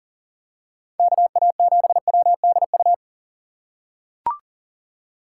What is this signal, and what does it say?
Morse code, heard as a keyed tone, about 40 wpm, 700 Hz. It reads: KA7WDU
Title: USA Call Signs - 6 Characters - 6x Character Spacing - 40wpm